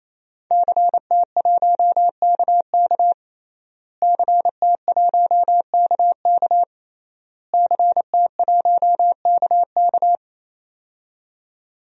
Morse code, heard as a keyed tone, about 28 wpm, 700 Hz. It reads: CT1KK CT1KK CT1KK
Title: DXCC-100 Call Signs - 5 Characters - Repeated 3x - 28wpm